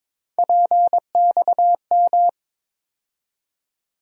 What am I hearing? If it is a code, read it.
Morse code, heard as a keyed tone, about 22 wpm, 700 Hz. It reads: PXM
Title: USA Call Sign Postfixes - 3 Letters 22wpm